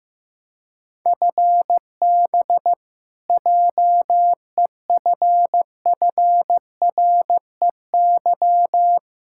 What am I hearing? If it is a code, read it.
Morse code, heard as a keyed tone, about 15 wpm, 700 Hz. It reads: FB JEFFREY
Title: K1USN SST QSO Practice 15wpm